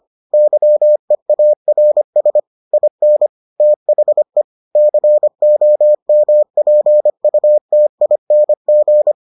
Morse code, heard as a keyed tone, about 25 wpm, 600 Hz. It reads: YEARS IN THE COMPUTING